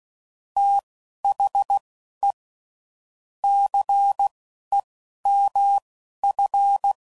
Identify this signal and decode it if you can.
Morse code, heard as a keyed tone, about 16 wpm, spaced as for 11 wpm, 800 Hz. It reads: THE CEMF
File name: CW Code Groups @10WPM / Technical Subjects / XL